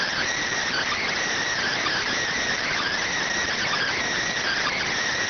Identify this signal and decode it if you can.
Emergency Alert System (SAME) tones (header bursts), heard as harsh buzzing bursts. It is heard from 0.7 to 4.9 s.
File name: findings / CT CORONAL 710mm 10mm scout.gen-35 rx